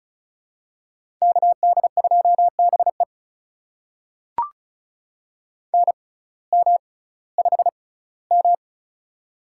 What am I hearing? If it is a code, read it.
Morse code, heard as a keyed tone, about 36 wpm, spaced as for 19 wpm, 700 Hz. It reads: KD2BE NM5M
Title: USA Call Signs - 6x Character Spacing - 35wpm